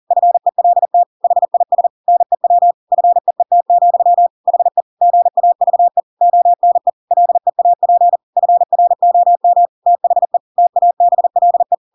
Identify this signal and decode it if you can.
Morse code, heard as a keyed tone, about 40 wpm, 700 Hz. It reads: FELT HIS NEW FEET, HE GAVE ONE LEAP FROM THE TABLE
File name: Pinocchio / Pinocchio ch8 40 40 ik1hge